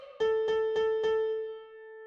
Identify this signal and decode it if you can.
There is no code.